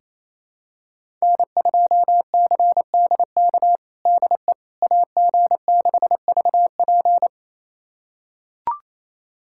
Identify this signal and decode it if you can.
Morse code, heard as a keyed tone, about 28 wpm, 700 Hz. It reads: N2CDK DE AG6VP
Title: USA Callsign DE Callsign - 5 Characters - 28wpm